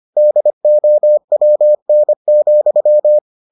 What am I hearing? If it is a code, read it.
Morse code, heard as a keyed tone, about 25 wpm, 600 Hz. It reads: DOWN,